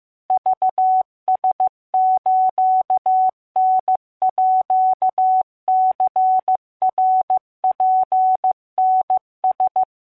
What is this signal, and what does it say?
Morse code (Japanese Wabun code), heard as a keyed tone, about 15 wpm, 750 Hz. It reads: クラスターニナツタラ